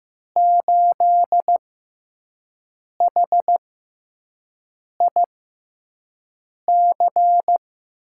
Morse code, heard as a keyed tone, about 15 wpm, spaced as for 5 wpm, 700 Hz. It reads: 8HIC
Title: USA Call Signs - 5 Characters - 6x Character Spacing - 15wpm